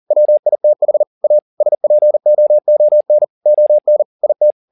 Morse code, heard as a keyed tone, about 40 wpm, 600 Hz. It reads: WITH A SPOON ON IT
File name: the old man and the sea 2 40wpm